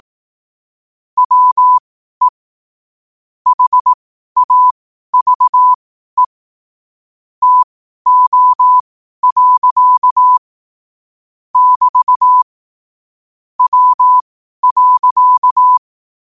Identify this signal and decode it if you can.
Morse code, heard as a keyed tone, about 18 wpm, spaced as for 12 wpm, 1000 Hz. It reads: WE HAVE TO. <BT> W.
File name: qotdfiles / 12wpm